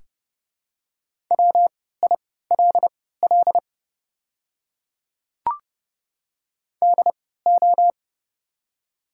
Morse code, heard as a keyed tone, about 31 wpm, spaced as for 17 wpm, 700 Hz. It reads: WILL DO
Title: 3x Character Spacing - Top 100 Words - 30wpm